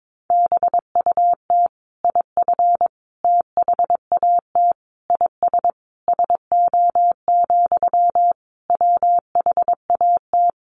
Morse code, heard as a keyed tone, about 22 wpm, 700 Hz. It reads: BUT IF THAT IS SO, WHAT
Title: Sentences from Top 200 Words 22wpm